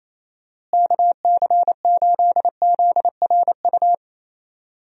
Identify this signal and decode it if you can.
Morse code, heard as a keyed tone, about 28 wpm, 700 Hz. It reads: KC8ZRU
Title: USA Call Signs - 6 Characters - Repeated 3x - 28wpm